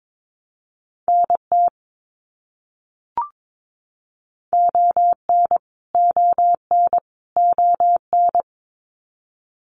Morse code, heard as a keyed tone, about 22 wpm, 700 Hz. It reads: NT ON ON ON
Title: Canadian Provinces and Territories - Repeated 3x - 22wpm